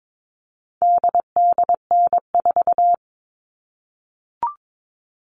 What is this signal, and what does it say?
Morse code, heard as a keyed tone, about 22 wpm, 700 Hz. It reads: DDN4
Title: All Cut Numbers - 4-Digit 22wpm